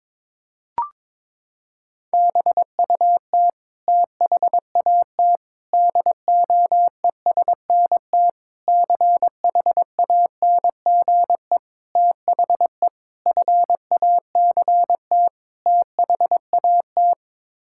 Morse code, heard as a keyed tone, about 22 wpm, 700 Hz. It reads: BUT THAT DOESNT CHANGE THE FACT THAT
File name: practice / 22wpm / Sets of 7 Words - 22wpm